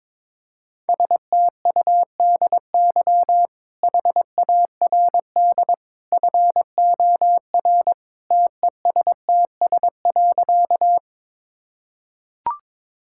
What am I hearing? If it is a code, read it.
Morse code, heard as a keyed tone, about 21 wpm, 700 Hz. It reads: STUDY HARD FOR TESTS.
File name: practice / 22wpm / Easy Sentences with Hints - 22wpm